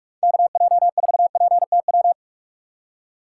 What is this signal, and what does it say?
Morse code, heard as a keyed tone, about 43 wpm, 700 Hz. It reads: KJ4PTW
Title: USA Call Signs - 6 Characters - 6x Character Spacing - 45wpm